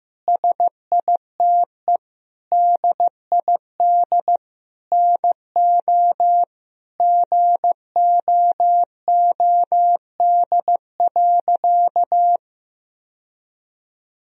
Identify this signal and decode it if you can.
Morse code, heard as a keyed tone, about 15 wpm, 700 Hz. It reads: SITE DID NO GOOD.
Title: Sentences from Top 1000 Words - Review Part 4 - 15wpm